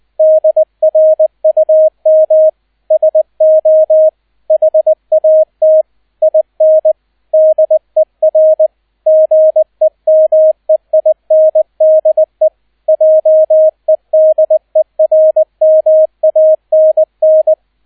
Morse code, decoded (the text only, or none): DRUM SO HAT IN DER GEMEINDE JEDERMANN